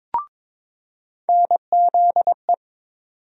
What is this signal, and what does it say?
Morse code, heard as a keyed tone, about 21 wpm, 700 Hz. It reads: NZE
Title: USA Call Sign Postfixes - 3 Letters 22wpm